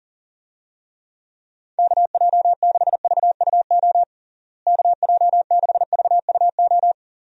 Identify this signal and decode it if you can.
Morse code, heard as a keyed tone, about 40 wpm, 700 Hz. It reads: KJ6VUO KJ6VUO
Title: USA Call Signs - 6 Characters - Repeated 3x - 40wpm